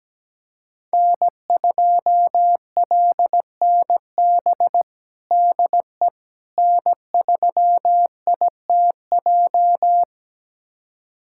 Morse code, heard as a keyed tone, about 17 wpm, 700 Hz. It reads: N2LNB DE N3ITJ
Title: USA Callsign DE Callsign - 5 Characters - 17wpm